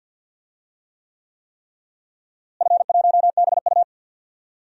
Morse code, heard as a keyed tone, about 50 wpm, 700 Hz. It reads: F1BU
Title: DXCC-100 Call Signs - 4 Characters - Repeated 3x - 50wpm